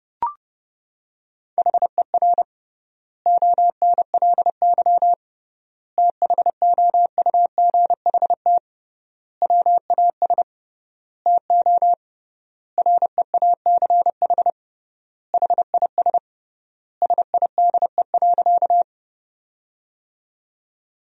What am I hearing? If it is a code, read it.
Morse code, heard as a keyed tone, about 31 wpm, 700 Hz. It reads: HER ONLY THOUGHT WAS TO REACH HIS SIDE.
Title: 3x Word-Spacing - Sentences from Top 400 Words - 30wpm